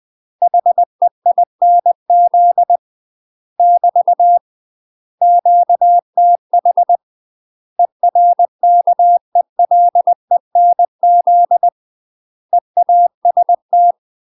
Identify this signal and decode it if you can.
Morse code, heard as a keyed tone, about 20 wpm, 700 Hz. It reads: HEINZ <BT> QTH ERKELENZ EAST
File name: UK MASTER QSOs TRAINING/20WPM